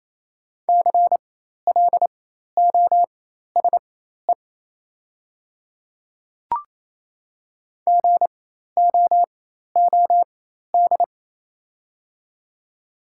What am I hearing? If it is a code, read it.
Morse code, heard as a keyed tone, about 28 wpm, spaced as for 13 wpm, 700 Hz. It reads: CLOSE GOOD
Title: 4x Character Spacing - Top 300 Words - 28wpm